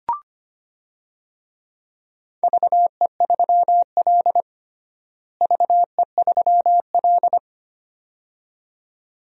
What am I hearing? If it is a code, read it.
Morse code, heard as a keyed tone, about 25 wpm, 700 Hz. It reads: VE3L VE3L
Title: DXCC-100 Call Signs - 4 Characters - Repeated 2x - 25wpm